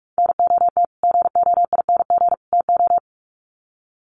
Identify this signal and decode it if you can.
Morse code, heard as a keyed tone, about 44 wpm, 700 Hz. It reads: NOT GOING TO
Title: Sets of 3 Words 45wpm